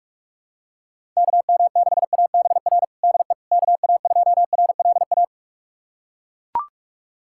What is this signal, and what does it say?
Morse code, heard as a keyed tone, about 42 wpm, 700 Hz. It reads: KM6ABR DE KA2RLA
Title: USA Callsign DE Callsign 45wpm